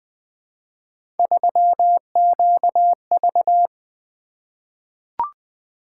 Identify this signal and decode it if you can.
Morse code, heard as a keyed tone, about 20 wpm, 700 Hz. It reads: 3QV
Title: Three Letter-Number Combinations - 20wpm